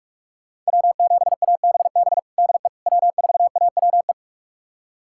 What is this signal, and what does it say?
Morse code, heard as a keyed tone, about 43 wpm, 700 Hz. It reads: W7ABB DE W4AWE
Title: USA Callsign DE Callsign - 5 Characters - 45wpm